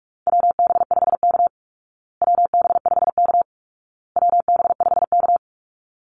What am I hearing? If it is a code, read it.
Morse code, heard as a keyed tone, about 44 wpm, 700 Hz. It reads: WB5K WB5K WB5K
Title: Call Signs Repeated 2x - 45wpm